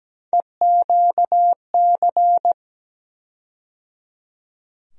Morse code, heard as a keyed tone, about 17 wpm, 700 Hz. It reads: EQC
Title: USA Call Sign Postfixes - 3 Letters 17wpm